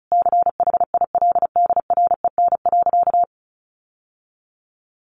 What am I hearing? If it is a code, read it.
Morse code, heard as a keyed tone, about 34 wpm, 700 Hz. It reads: CHILDREN.
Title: Sentences from Top 1500 Words 35wpm